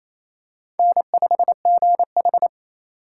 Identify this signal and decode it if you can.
Morse code, heard as a keyed tone, about 28 wpm, 700 Hz. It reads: N5GH
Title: USA Call Signs - 4 Characters - 3x Character Spacing - 28wpm